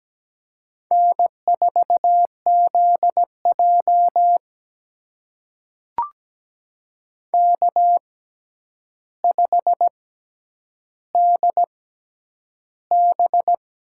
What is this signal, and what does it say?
Morse code, heard as a keyed tone, about 17 wpm, spaced as for 9 wpm, 700 Hz. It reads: N4ZJ K5DB
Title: USA Call Signs - 6x Character Spacing - 17wpm